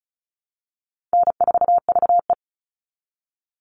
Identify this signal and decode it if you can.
Morse code, heard as a keyed tone, about 34 wpm, 700 Hz. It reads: N4VE
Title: All Cut Numbers - 4-Digit 35wpm